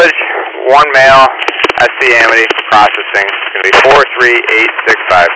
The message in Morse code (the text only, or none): none